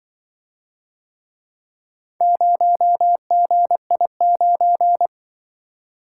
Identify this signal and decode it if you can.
Morse code, heard as a keyed tone, about 25 wpm, 700 Hz. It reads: 0GI9
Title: Speed-Racing - Four Letter-Number Combinations - 20wpm